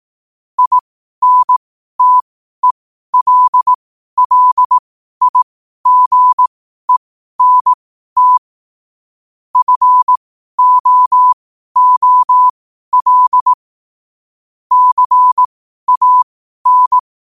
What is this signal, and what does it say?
Morse code, heard as a keyed tone, about 18 wpm, spaced as for 12 wpm, 1000 Hz. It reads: INTELLIGENT FOOL CAN
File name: qotdfiles / 12wpm